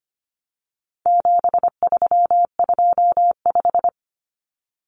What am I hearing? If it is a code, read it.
Morse code, heard as a keyed tone, about 25 wpm, 700 Hz. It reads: 7325
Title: Numbers - 4-Digit 25wpm